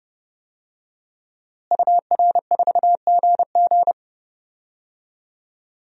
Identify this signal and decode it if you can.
Morse code, heard as a keyed tone, about 30 wpm, 700 Hz. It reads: UR4GG
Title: DXCC-100 Call Signs - 5 Characters - Repeated 3x - 30wpm